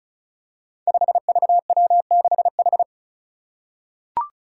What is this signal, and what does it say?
Morse code, heard as a keyed tone, about 36 wpm, 700 Hz. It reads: 5VW6H
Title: Five Challenging Letter-Number Combinations 35wpm